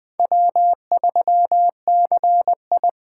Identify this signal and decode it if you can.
Morse code, heard as a keyed tone, about 20 wpm, 700 Hz. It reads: W3CI
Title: Speed Racing - USA 1x2 Call Signs 17wpm